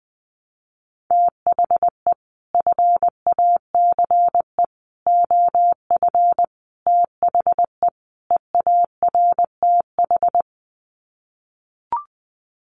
Morse code, heard as a keyed tone, about 20 wpm, 700 Hz. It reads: THE FACE OF THE EARTH
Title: Sets of 5 Words Encore 20wpm